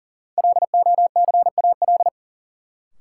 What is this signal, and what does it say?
Morse code, heard as a keyed tone, about 40 wpm, 700 Hz. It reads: LOCAL
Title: Sentences from Top 1000 Words - Review Part 3 - New Word Drill - 40wpm